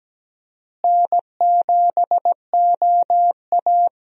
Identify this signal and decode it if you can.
Morse code, heard as a keyed tone, about 17 wpm, 700 Hz. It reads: N7OA
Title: USA Call Signs - 4 Characters - Repeated 3x - 17wpm